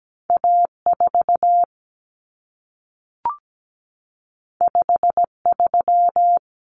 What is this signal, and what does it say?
Morse code, heard as a keyed tone, about 17 wpm, 700 Hz. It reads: A4 53
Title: Common Cut Numbers - 2-Digit 17wpm